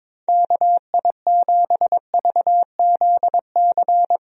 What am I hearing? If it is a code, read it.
Morse code, heard as a keyed tone, about 22 wpm, 700 Hz. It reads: KI7VZC
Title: USA Call Signs - 6 Characters - 22wpm